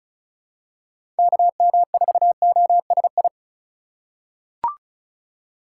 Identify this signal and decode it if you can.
Morse code, heard as a keyed tone, about 34 wpm, 700 Hz. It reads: KM4OSI
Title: USA Call Signs - 6x Character Spacing - 35wpm